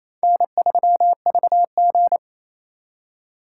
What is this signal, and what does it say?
Morse code, heard as a keyed tone, about 34 wpm, 700 Hz. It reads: N3VG
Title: Speed Racing - USA 1x2 Call Signs 28wpm